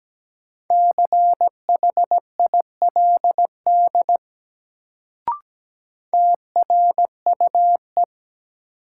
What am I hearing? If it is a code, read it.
Morse code, heard as a keyed tone, about 17 wpm, 700 Hz. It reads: CHILD TRUE
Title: Sets of 5 Words Encore - New Word Drill - 17wpm